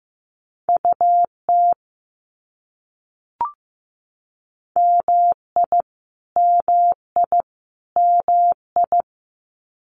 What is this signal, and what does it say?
Morse code, heard as a keyed tone, about 15 wpm, 700 Hz. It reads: UT MI MI MI
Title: US States Repeated 3x 15wpm